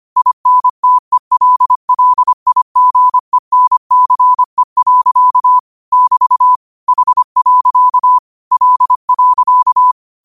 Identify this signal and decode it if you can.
Morse code, heard as a keyed tone, about 25 wpm, 1000 Hz. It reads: INTELLIGENCE. <BT> H. L.
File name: qotdfiles / 25wpm